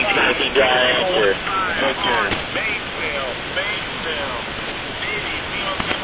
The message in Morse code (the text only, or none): none